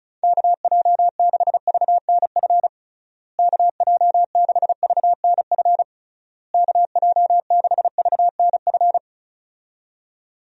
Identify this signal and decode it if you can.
Morse code, heard as a keyed tone, about 36 wpm, 700 Hz. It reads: KJ6VNF KJ6VNF KJ6VNF
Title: USA Call Signs - 6 Characters - Repeated 3x - 35wpm